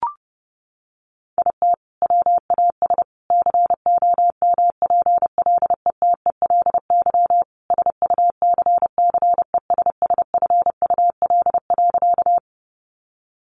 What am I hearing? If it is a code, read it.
Morse code, heard as a keyed tone, about 30 wpm, 700 Hz. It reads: IT WAS COMPLETELY SUCCESSFUL.
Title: Sentences from Top 1100 Words 30wpm